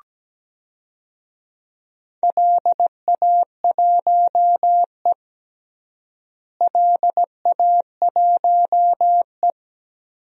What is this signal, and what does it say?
Morse code, heard as a keyed tone, about 17 wpm, 700 Hz. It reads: LA1E LA1E
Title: DXCC-100 Call Signs - 4 Characters - Repeated 3x - 17wpm